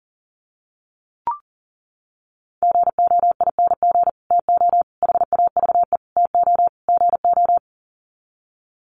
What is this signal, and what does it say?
Morse code, heard as a keyed tone, about 37 wpm, 700 Hz. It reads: GOING TO HAVE TO GO
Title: Sets of 5 Words Encore 40wpm